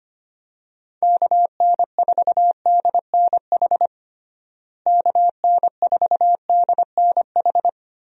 Morse code, heard as a keyed tone, about 25 wpm, 700 Hz. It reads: KN4DNH KN4DNH
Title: USA Call Signs - 6 Characters - Repeated 3x - 25wpm